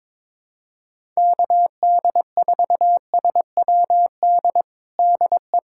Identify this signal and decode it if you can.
Morse code, heard as a keyed tone, about 22 wpm, 700 Hz. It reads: KD4SWD DE
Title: USA Callsign DE Callsign - 6 Characters - 22wpm